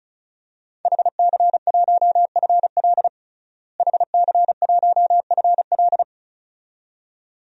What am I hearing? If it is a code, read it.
Morse code, heard as a keyed tone, about 34 wpm, 700 Hz. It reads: HC1FL HC1FL
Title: DXCC-100 Call Signs - 5 Characters - Repeated 2x - 35wpm